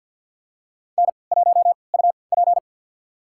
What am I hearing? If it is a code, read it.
Morse code, heard as a keyed tone, about 51 wpm, spaced as for 28 wpm, 700 Hz. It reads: N1UP
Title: USA Call Signs - 3x Character Spacing - 50wpm